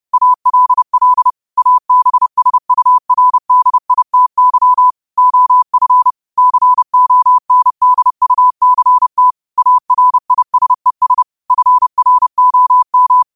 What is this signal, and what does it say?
Morse code, heard as a keyed tone, about 30 wpm, 1000 Hz. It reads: ALL ABSURDITY OF CONDUCT ARISES FROM